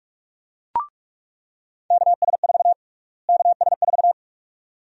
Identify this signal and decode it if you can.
Morse code, heard as a keyed tone, about 46 wpm, 700 Hz. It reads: KS4 KS4
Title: USA Call Sign Prefixes - Repeated 2x - 45wpm